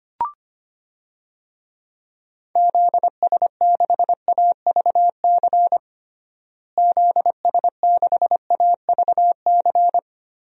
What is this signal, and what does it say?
Morse code, heard as a keyed tone, about 25 wpm, 700 Hz. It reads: ZS6AVC ZS6AVC
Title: DXCC-100 Call Signs - 6 Characters - Repeated 3x - 25wpm